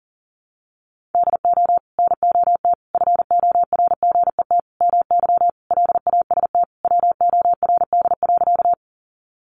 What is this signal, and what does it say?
Morse code, heard as a keyed tone, about 40 wpm, 700 Hz. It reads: DO NOT FORGET MY LAST WORD.
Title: Sentences from Top 1200 Words 40wpm